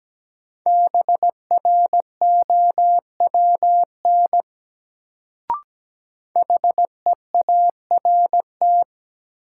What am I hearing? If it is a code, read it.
Morse code, heard as a keyed tone, about 17 wpm, 700 Hz. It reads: BROWN HEART
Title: Sentences from Top 1200 Words - New Word Drill Alt - 17wpm